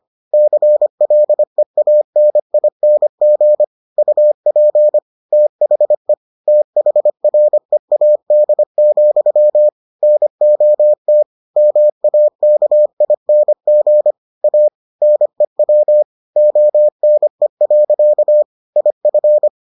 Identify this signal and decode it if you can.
Morse code, heard as a keyed tone, about 25 wpm, 600 Hz. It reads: CLEANING UP THE THREAD, NOT MAKING A NEW ONE. IF